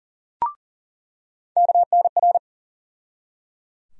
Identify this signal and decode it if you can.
Morse code, heard as a keyed tone, about 40 wpm, 700 Hz. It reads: KNR
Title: USA Call Sign Postfixes - 3 Letters 40wpm